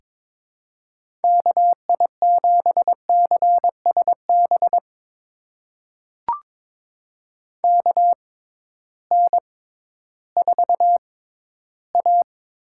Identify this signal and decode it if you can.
Morse code, heard as a keyed tone, about 22 wpm, spaced as for 13 wpm, 700 Hz. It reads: KI7CSB KN4A
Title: USA Call Signs - 6 Characters - 6x Character Spacing - 22wpm